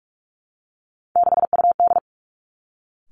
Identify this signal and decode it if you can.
Morse code, heard as a keyed tone, about 46 wpm, 700 Hz. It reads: 6UD
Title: Three Challenging Letter-Number Combinations 45wpm